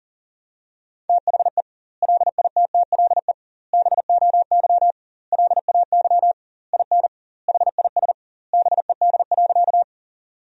Morse code, heard as a keyed tone, about 37 wpm, 700 Hz. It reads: THE LITTLE BOY LAY IN HIS BED.
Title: 2x Word-Spacing - Sentences from Top 500 Words - 40wpm